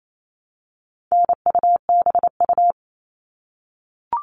Morse code, heard as a keyed tone, about 28 wpm, 700 Hz. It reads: NUBU
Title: All Cut Numbers - 4-Digit 28wpm